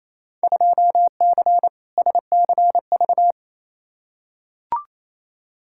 Morse code, heard as a keyed tone, about 28 wpm, 700 Hz. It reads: 2C SCV
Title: Field Day Exchanges Encore - 28wpm